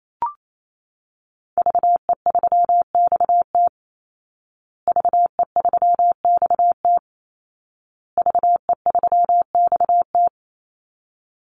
Morse code, heard as a keyed tone, about 28 wpm, 700 Hz. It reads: VE3XT VE3XT VE3XT
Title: Call Signs Repeated 2x - 28wpm